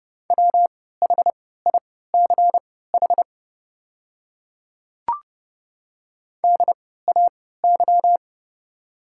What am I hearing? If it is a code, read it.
Morse code, heard as a keyed tone, about 32 wpm, spaced as for 17 wpm, 700 Hz. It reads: WHICH DAY
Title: 3x Character Spacing - Top 100 Words - 30wpm